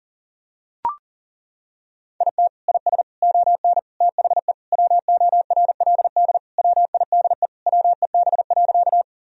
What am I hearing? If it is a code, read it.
Morse code, heard as a keyed tone, about 37 wpm, 700 Hz. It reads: IT IS ON THE WORLD WIDE WEB.
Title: Sentences from Top 1000 Words - Review Part 4 - 40wpm